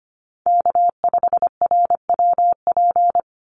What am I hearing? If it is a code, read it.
Morse code, heard as a keyed tone, about 25 wpm, 700 Hz. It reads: K5RWP
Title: NA SOTA Chaser Callsigns 25wpm